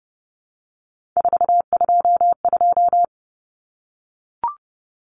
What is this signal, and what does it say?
Morse code, heard as a keyed tone, about 30 wpm, 700 Hz. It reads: 422
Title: Common Cut Numbers - 3-Digit 30wpm